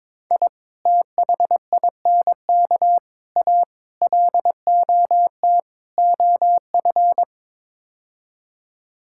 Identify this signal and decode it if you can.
Morse code, heard as a keyed tone, about 22 wpm, 700 Hz. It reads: I THINK A LOT OF
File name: practice / 22wpm / Build A Phrase - 5-Word Phrases - 22wpm